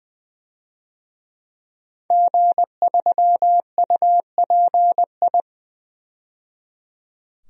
Morse code, heard as a keyed tone, about 20 wpm, 700 Hz. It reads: G3UPI
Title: DXCC-100 Call Signs - 5 Characters - 20wpm